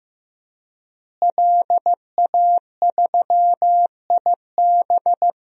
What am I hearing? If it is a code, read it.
Morse code, heard as a keyed tone, about 15 wpm, 700 Hz. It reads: LA3IB
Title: DXCC-100 Call Signs - 5 Characters - Repeated 3x - 15wpm